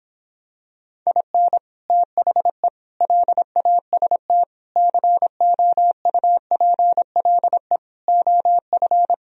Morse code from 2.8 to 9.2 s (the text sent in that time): LAST COUPLE OF